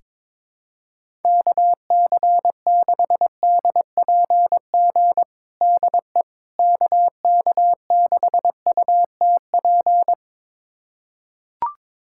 Morse code, heard as a keyed tone, about 22 wpm, 700 Hz. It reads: KC6DPG DE KK6UTP
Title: USA Callsign DE Callsign - 6 Characters - 22wpm